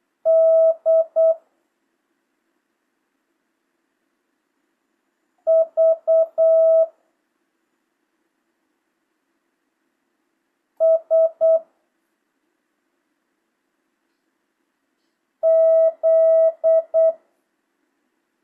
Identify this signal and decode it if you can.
Morse code, heard as a keyed tone, about 8 wpm, 650 Hz. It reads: D V S Z
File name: jh8cbh / 2025cw